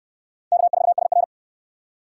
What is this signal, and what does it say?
Morse code, heard as a keyed tone, about 60 wpm, 700 Hz. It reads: K4SR